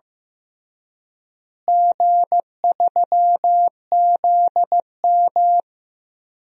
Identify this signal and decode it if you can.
Morse code, heard as a keyed tone, about 15 wpm, 700 Hz. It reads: G3ZM
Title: DXCC-100 Call Signs - 4 Characters - Repeated 3x - 15wpm